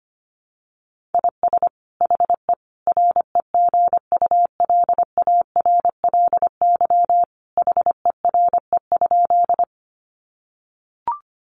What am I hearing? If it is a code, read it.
Morse code, heard as a keyed tone, about 25 wpm, 700 Hz. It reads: IS HE REGULARLY HERE?
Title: Sentences from Top 1400 Words 25wpm